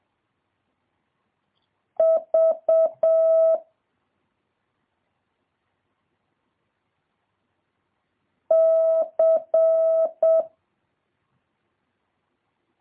Morse code, heard as a keyed tone, about 7 wpm, 650 Hz. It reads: V C
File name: jh8cbh / 2023cw